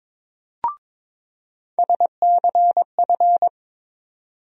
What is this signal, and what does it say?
Morse code, heard as a keyed tone, about 22 wpm, 700 Hz. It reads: SCF
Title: USA Call Sign Postfixes - 3 Letters 22wpm